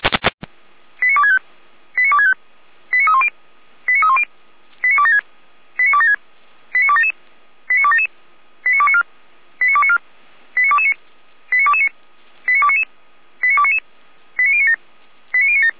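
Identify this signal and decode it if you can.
Five-tone selective call (ZVEI), heard as quick runs of five tones, tones of about 80 ms. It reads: 89265 89265 89310 89310 89276 89276 89277 89277 89294 89294 89229 89229 89299 89299 89097 89097
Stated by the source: Five-tone ZVEI selective call 89075, received on SDR.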